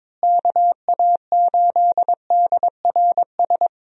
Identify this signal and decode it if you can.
Morse code, heard as a keyed tone, about 23 wpm, 700 Hz. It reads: KA8DRS
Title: USA Call Signs - 6 Characters - Repeated 2x - 22wpm